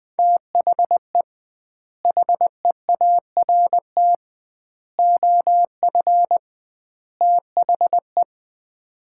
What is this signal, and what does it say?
Morse code, heard as a keyed tone, about 20 wpm, 700 Hz. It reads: THE HEART OF THE
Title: 2x Word-Spacing - Sets of 4 Words - 20wpm